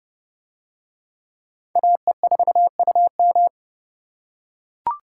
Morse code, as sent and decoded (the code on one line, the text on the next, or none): .- . ....- ..- --
AE4UM